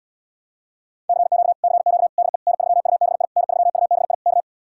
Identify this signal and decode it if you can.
Morse code, heard as a keyed tone, about 77 wpm, 700 Hz. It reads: CQ CQ DE N3UBI N3UBI K